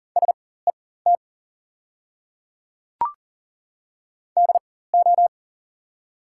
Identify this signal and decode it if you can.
Morse code, heard as a keyed tone, about 40 wpm, spaced as for 19 wpm, 700 Hz. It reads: SET DO
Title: 4x Character Spacing - Top 200 Words - 40wpm